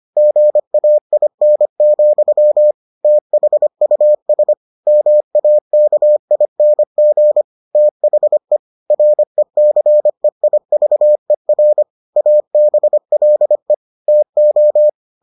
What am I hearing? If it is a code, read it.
Morse code, heard as a keyed tone, about 25 wpm, 600 Hz. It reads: GAIN, THUS MAKING THE RECEIVER ABLE TO